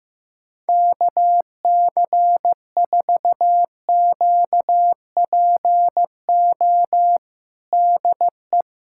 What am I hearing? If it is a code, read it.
Morse code, heard as a keyed tone, about 16 wpm, 700 Hz. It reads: KC4QPO DE
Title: USA Callsign DE Callsign 15wpm